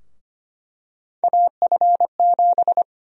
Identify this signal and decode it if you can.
Morse code, heard as a keyed tone, about 25 wpm, 700 Hz. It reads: AF7
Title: USA Call Sign Prefixes - Repeated 2x - 25wpm